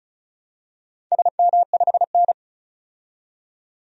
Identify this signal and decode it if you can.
Morse code, heard as a keyed tone, about 36 wpm, 700 Hz. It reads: SM5N
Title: DXCC-100 Call Signs - 4 Characters - 35wpm